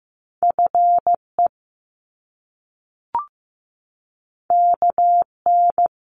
Morse code, heard as a keyed tone, about 15 wpm, 700 Hz. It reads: FE KN
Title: Top Two Letter Combinations in Words 15wpm